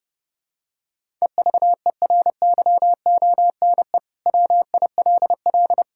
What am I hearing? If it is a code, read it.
Morse code, heard as a keyed tone, about 32 wpm, 700 Hz. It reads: EVERYONE WILL
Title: Sentences from Top 1000 Words - Review Part 5 - 30wpm